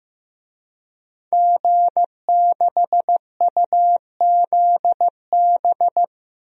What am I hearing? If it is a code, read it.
Morse code, heard as a keyed tone, about 15 wpm, 700 Hz. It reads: G6UZB
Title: DXCC-100 Call Signs - 5 Characters - 15wpm